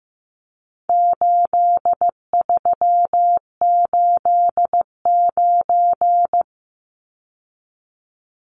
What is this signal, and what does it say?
Morse code, heard as a keyed tone, about 15 wpm, 700 Hz. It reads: 8389
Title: Numbers - 4-Digit 15wpm